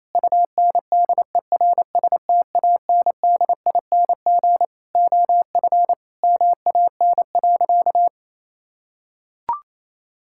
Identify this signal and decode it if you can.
Morse code, heard as a keyed tone, about 28 wpm, 700 Hz. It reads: UNDERSTANDING OF MAN.
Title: Sentences from Top 1000 Words - Review Part 5 - 28wpm